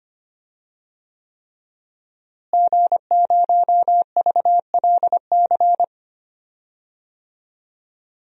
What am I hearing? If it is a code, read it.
Morse code, heard as a keyed tone, about 25 wpm, 700 Hz. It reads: G0VLC
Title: DXCC-100 Call Signs - 5 Characters - Repeated 3x - 25wpm